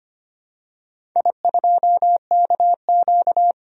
Morse code, heard as a keyed tone, about 25 wpm, 690 Hz. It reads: I2KQ
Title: DXCC-100 Call Signs - 4 Characters - 25wpm